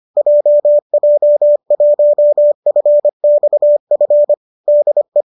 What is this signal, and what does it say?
Morse code, heard as a keyed tone, about 25 wpm, 590 Hz. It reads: JJ1FXF DE
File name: qso208 25